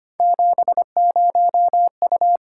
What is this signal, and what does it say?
Morse code, heard as a keyed tone, about 25 wpm, 700 Hz. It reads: 70U
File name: Three Letter-Number Combinations - 25wpm